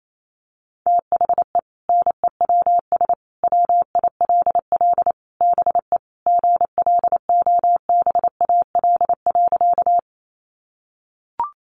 Morse code, heard as a keyed tone, about 28 wpm, 700 Hz. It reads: THE NEWS WILL BE GLOBAL.
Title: Sentences from Top 1200 Words 28wpm